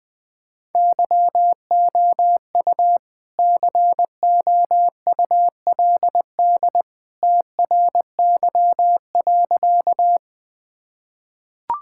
Morse code, heard as a keyed tone, about 20 wpm, 700 Hz. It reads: YOU COULD TRY.
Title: Sentences with 2-Word Phrases - Spoken Beginning - 20wpm